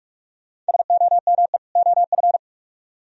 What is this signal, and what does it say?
Morse code, heard as a keyed tone, about 63 wpm, spaced as for 54 wpm, 700 Hz. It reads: SOME OF
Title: Speed-Racing - Sets of 2 Words 45wpm